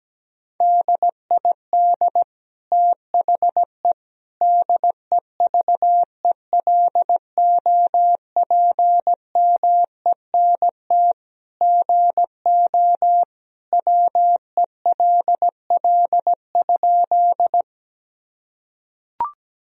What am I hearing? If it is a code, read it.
Morse code, heard as a keyed tone, about 17 wpm, 700 Hz. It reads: DID THE DEVELOPMENT GO WELL?
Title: Sentences from Top 1000 Words - Review Part 4 - 17wpm